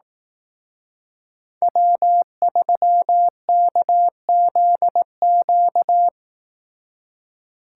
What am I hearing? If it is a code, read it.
Morse code, heard as a keyed tone, about 18 wpm, 700 Hz. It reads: W3KZQ
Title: Speed Racing - USA 1x3 Call Signs 15wpm